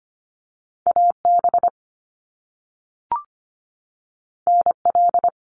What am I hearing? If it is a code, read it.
Morse code, heard as a keyed tone, about 25 wpm, 700 Hz. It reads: AB NL
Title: Canadian Provinces and Territories 25wpm